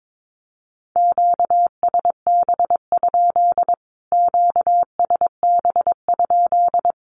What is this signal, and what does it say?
Morse code, heard as a keyed tone, about 22 wpm, 700 Hz. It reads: QSB? QSB?
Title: Q-Codes Repeated 3x 22wpm